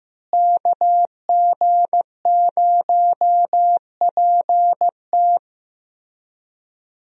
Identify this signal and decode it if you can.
Morse code, heard as a keyed tone, about 16 wpm, 700 Hz. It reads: KG0PT
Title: USA Call Signs - 5 Characters - Repeated 3x - 15wpm